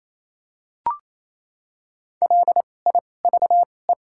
Morse code, heard as a keyed tone, about 28 wpm, spaced as for 20 wpm, 700 Hz. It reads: LIVE
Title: 2x Character Spacing - Top 300 Words - 28wpm